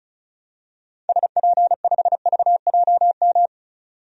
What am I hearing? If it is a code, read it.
Morse code, heard as a keyed tone, about 36 wpm, 700 Hz. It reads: SP5VJM
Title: DXCC-100 Call Signs - 6 Characters - 35wpm